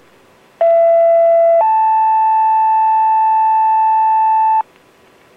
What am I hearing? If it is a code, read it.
Two-tone page, heard as a one-second tone, then a three-second tone, about 670 then 880 Hz, with noise. It is heard from 0.6 to 4.6 s.